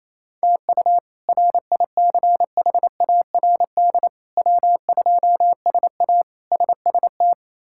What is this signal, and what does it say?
Morse code, heard as a keyed tone, about 28 wpm, 700 Hz. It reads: TU RICHARD W2SA SST
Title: K1USN SST QSO Practice - 28wpm